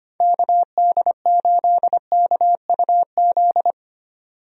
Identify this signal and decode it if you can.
Morse code, heard as a keyed tone, about 25 wpm, 700 Hz. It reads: KD8KUZ
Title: USA Call Signs - 6 Characters - 25wpm